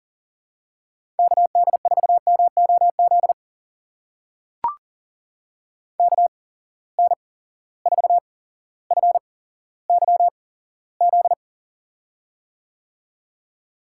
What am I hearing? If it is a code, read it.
Morse code, heard as a keyed tone, about 40 wpm, spaced as for 17 wpm, 700 Hz. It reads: KD4MOZ KN4FYZ
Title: USA Call Signs - 6 Characters - 8x Character Spacing - 40wpm